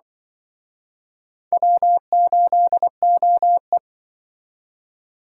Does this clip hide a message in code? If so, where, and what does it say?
Yes: Morse code from 1.5 to 3.8 s, heard as a keyed tone, about 25 wpm, 700 Hz. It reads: W8OE